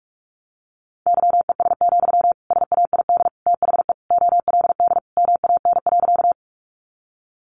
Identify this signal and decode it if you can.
Morse code, heard as a keyed tone, about 44 wpm, 700 Hz. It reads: YES, SAID THE OLD MAN.
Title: Sentences from Top 500 Words 45wpm